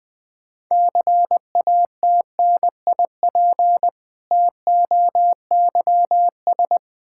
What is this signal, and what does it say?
Morse code, heard as a keyed tone, about 20 wpm, 700 Hz. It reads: CATNIP TOYS
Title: Difficult Words with Hints - 20wpm